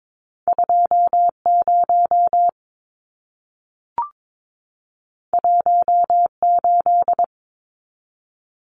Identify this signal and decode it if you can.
Morse code, heard as a keyed tone, about 22 wpm, 700 Hz. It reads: 20 18
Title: Numbers - 2-Digit 22wpm